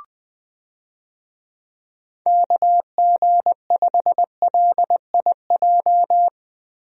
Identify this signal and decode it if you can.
Morse code, heard as a keyed tone, about 20 wpm, 700 Hz. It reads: KG5LIJ
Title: USA Call Signs - 6 Characters - Repeated 3x - 20wpm